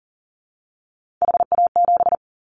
Morse code, heard as a keyed tone, about 40 wpm, 700 Hz. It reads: HA7
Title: Callsign Prefixes for Top 25 DXCC - 40wpm